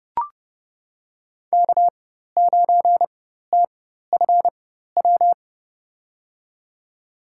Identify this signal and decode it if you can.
Morse code, heard as a keyed tone, about 31 wpm, spaced as for 14 wpm, 700 Hz. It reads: K9TFW
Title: USA Call Signs - 5 Characters - 4x Character Spacing - 30wpm